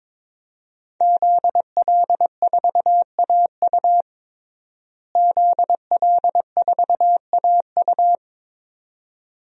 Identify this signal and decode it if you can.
Morse code, heard as a keyed tone, about 21 wpm, 700 Hz. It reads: ZL4AU ZL4AU
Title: DXCC-100 Call Signs - 5 Characters - Repeated 2x - 22wpm